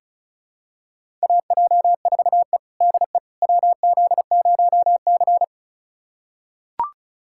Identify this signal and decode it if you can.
Morse code, heard as a keyed tone, about 36 wpm, 700 Hz. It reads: AJ4E DE WZ0C
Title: USA Callsign DE Callsign - 4 Characters - 35wpm